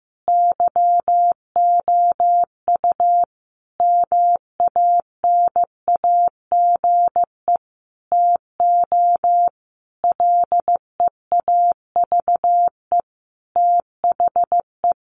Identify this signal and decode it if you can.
Morse code, heard as a keyed tone, about 15 wpm, 700 Hz. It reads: YOU MANAGE TO LEAVE THE
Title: Sentences from Top 1400 Words 15wpm